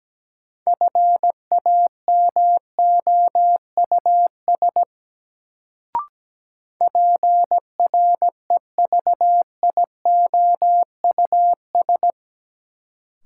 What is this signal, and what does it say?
Morse code, heard as a keyed tone, about 17 wpm, 700 Hz. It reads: FAMOUS PREVIOUS
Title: Sentences from Top 1500 Words - New Word Drill - 17wpm